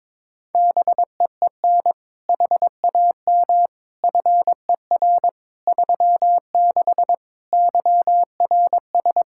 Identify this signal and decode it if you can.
Morse code, heard as a keyed tone, about 22 wpm, 700 Hz. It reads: BEEN HAM FER 36 YRS
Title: Practice QSO - Exchange 3 - More Info - 22wpm